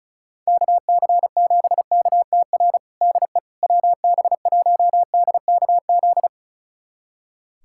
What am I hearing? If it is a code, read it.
Morse code, heard as a keyed tone, about 36 wpm, 700 Hz. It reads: KC7KTR DE WB1DKZ
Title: USA Callsign DE Callsign 35wpm